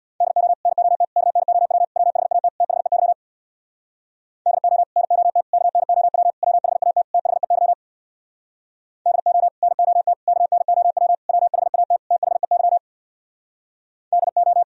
Morse code, heard as a keyed tone, about 62 wpm, 700 Hz. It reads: DO NOT KNOW WHAT THEY DO NOT KNOW WHAT THEY DO NOT KNOW WHAT THEY DO
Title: Speed-Racing - Sets of 5 Words 50wpm